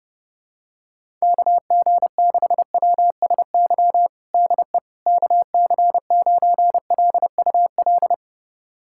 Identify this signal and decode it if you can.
Morse code, heard as a keyed tone, about 30 wpm, 700 Hz. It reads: KG6WSY DE KC9LUL
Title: USA Callsign DE Callsign - 6 Characters - 30wpm